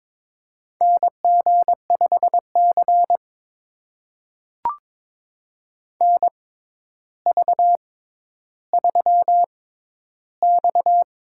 Morse code, heard as a keyed tone, about 22 wpm, spaced as for 11 wpm, 700 Hz. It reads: NG5C NV3X
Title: USA Call Signs - 4 Characters - 6x Character Spacing - 22wpm